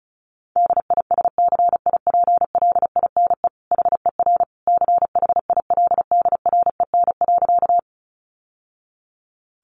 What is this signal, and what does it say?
Morse code, heard as a keyed tone, about 34 wpm, 700 Hz. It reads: DISCIPLINE HER CHILDREN.
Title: Sentences from Top 900 Words 35wpm